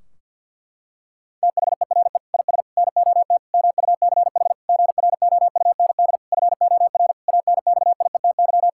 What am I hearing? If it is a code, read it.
Morse code, heard as a keyed tone, about 50 wpm, 700 Hz. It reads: THERE IS NOT MUCH GROUND FOR ANXIETY
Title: Sentences from Top 1600 Words 50wpm